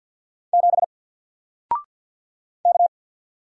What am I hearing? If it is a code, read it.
Morse code, heard as a keyed tone, about 48 wpm, 700 Hz. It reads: 7 K
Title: Single Letter-Number Alt 50wpm